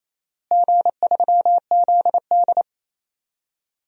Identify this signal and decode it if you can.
Morse code, heard as a keyed tone, about 28 wpm, 700 Hz. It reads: G3ZD
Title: DXCC-100 Call Signs - 4 Characters - Repeated 3x - 28wpm